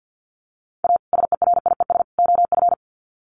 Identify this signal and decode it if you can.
Morse code, heard as a keyed tone, about 50 wpm, 700 Hz. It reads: A SERIES OF
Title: Sets of 3 Words 50wpm